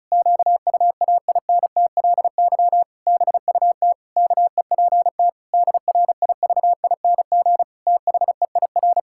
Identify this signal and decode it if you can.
Morse code, heard as a keyed tone, about 34 wpm, 700 Hz. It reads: QUAINTLY BUT KEPT DRIVING THEIR
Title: Pangram Challenge v2 - 35wpm